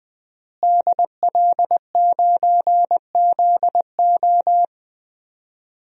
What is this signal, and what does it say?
Morse code, heard as a keyed tone, about 19 wpm, 700 Hz. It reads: DL9ZO
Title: DXCC-100 Call Signs - 5 Characters - Repeated 2x - 20wpm